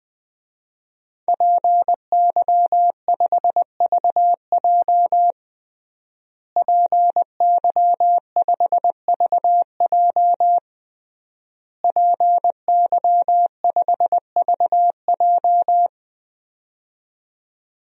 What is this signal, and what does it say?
Morse code, heard as a keyed tone, about 20 wpm, 700 Hz. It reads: PY5VJ PY5VJ PY5VJ
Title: DXCC-100 Call Signs - 5 Characters - Repeated 3x - 20wpm